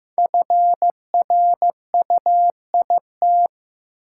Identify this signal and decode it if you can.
Morse code, heard as a keyed tone, about 15 wpm, 700 Hz. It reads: FRUIT
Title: Sentences from Top 1300 Words - New Word Drill Alt - 15wpm